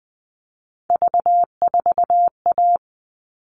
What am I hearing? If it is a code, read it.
Morse code, heard as a keyed tone, about 20 wpm, 700 Hz. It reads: V4A
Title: All Cut Numbers - 3-Digit 20wpm